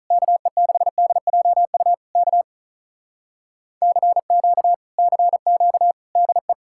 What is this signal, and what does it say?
Morse code, heard as a keyed tone, about 48 wpm, 700 Hz. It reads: KE6DJU K CQ CQ DE